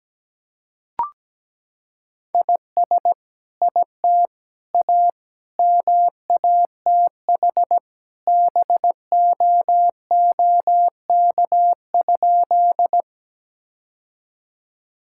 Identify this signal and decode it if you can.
Morse code, heard as a keyed tone, about 17 wpm, 700 Hz. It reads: IS IT A MATH BOOK?
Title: Sentences from Top 1000 Words - Review Part 6 - 17wpm